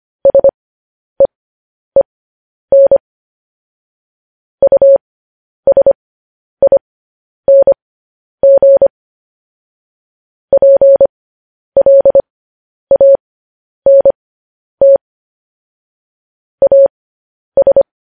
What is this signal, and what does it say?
Morse code, heard as a keyed tone, about 25 wpm, spaced as for 10 wpm, 550 Hz. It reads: SEEN USING PLANT AS
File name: podcast / News headlines-1x-25x10